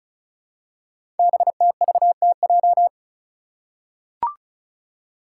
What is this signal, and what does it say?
Morse code, heard as a keyed tone, about 36 wpm, 690 Hz. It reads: BTVTJ
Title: Code Groups - 5 Letters - 35wpm